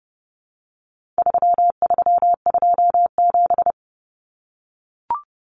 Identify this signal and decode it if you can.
Morse code, heard as a keyed tone, about 29 wpm, 700 Hz. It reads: 3327
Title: Common Cut Numbers - 4-Digit 30wpm